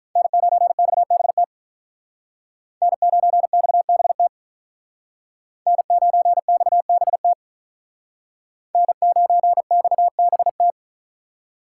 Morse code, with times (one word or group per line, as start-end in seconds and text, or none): 0.2-1.4 N9XBT
2.8-4.3 N9XBT
5.7-7.3 N9XBT
8.7-10.7 N9XBT